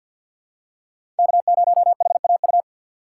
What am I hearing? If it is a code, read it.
Morse code, heard as a keyed tone, about 48 wpm, 700 Hz. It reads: K0HAU
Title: USA Call Signs - 3x Character Spacing - 50wpm